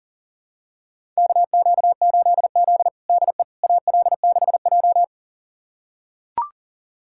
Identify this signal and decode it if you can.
Morse code, heard as a keyed tone, about 40 wpm, 700 Hz. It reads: KQ8Z DE AL6J
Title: USA Callsign DE Callsign - 4 Characters - 40wpm